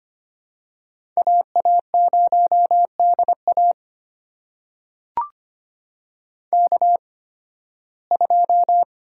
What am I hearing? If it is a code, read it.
Morse code, heard as a keyed tone, about 25 wpm, spaced as for 16 wpm, 700 Hz. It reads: AA0DA K2
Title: USA Call Signs - 5 Characters - 8x Character Spacing - 25wpm